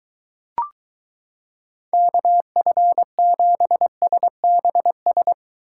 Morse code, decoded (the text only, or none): KF7SBS